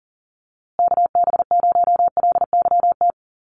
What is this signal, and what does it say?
Morse code, heard as a keyed tone, about 40 wpm, 700 Hz. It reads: KB0LYT
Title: USA Call Signs Repeated 3x 40wpm